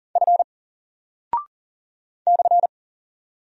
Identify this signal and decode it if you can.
Morse code, heard as a keyed tone, about 40 wpm, 700 Hz. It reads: F /
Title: Single Character (Letter:Number:Punctuation:Prosign) 40wpm